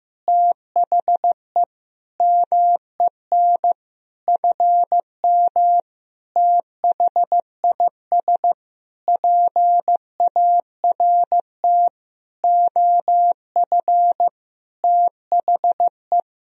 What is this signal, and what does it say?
Morse code, heard as a keyed tone, about 15 wpm, 700 Hz. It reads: THE MEN FM THIS PART OF THE
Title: Sentences from Top 300 Words and Abbreviations 15wpm